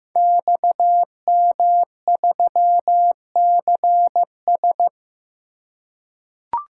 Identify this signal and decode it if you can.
Morse code, heard as a keyed tone, about 15 wpm, 700 Hz. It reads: XM3CS
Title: Code Groups - 5 Letter:Numbers - 15wpm